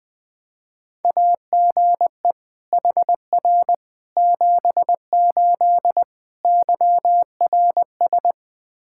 Morse code, heard as a keyed tone, about 20 wpm, 700 Hz. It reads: AGE HR 78 YRS
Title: Practice QSO - Exchange 3 - More Info - 20wpm